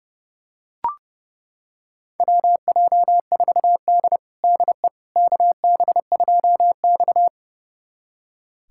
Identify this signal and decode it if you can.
Morse code, heard as a keyed tone, about 30 wpm, 700 Hz. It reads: WJ4D DE KB2X
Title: USA Callsign DE Callsign - 4 Characters - 30wpm